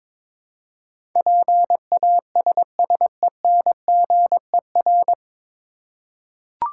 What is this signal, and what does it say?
Morse code, heard as a keyed tone, about 22 wpm, 700 Hz. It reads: PASSENGER
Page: Morse Code Ninja practice 22wpm